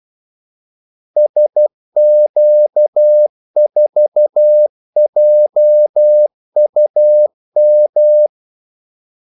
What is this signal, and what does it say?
Morse code, heard as a keyed tone, about 12 wpm, 600 Hz. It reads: SQ4JUM
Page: 2 ekstra ordmellemrum mellem kaldesignalerne